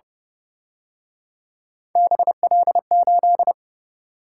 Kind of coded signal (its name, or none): Morse code